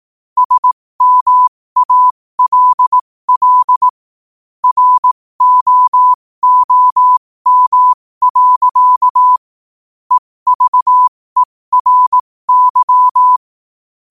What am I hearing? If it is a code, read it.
Morse code, heard as a keyed tone, about 18 wpm, spaced as for 15 wpm, 1000 Hz. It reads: SMALL ROOM. EVERY
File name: qotdfiles / 15wpm